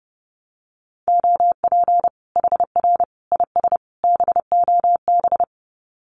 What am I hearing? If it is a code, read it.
Morse code, heard as a keyed tone, about 30 wpm, 700 Hz. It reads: OP HR IS BOB
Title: Common QSO Phrases 30wpm